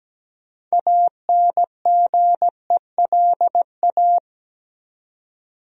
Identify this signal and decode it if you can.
Morse code, heard as a keyed tone, about 17 wpm, 700 Hz. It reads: ANGELA
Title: USA Common Names - 17wpm